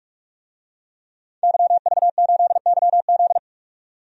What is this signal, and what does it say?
Morse code, heard as a keyed tone, about 47 wpm, 700 Hz. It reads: YV8YZ
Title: DXCC-100 Call Signs - 5 Characters - 45wpm